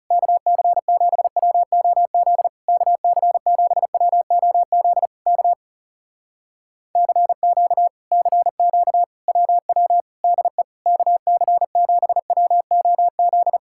Morse code, heard as a keyed tone, about 37 wpm, 700 Hz. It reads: KC7WOZ KC7WOZ K CQ CQ WW DE KC7WOZ
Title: Speed-Racing Common QSO Phrases Encore - 30wpm